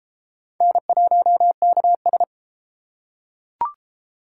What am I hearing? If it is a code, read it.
Morse code, heard as a keyed tone, about 31 wpm, 700 Hz. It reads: N1KS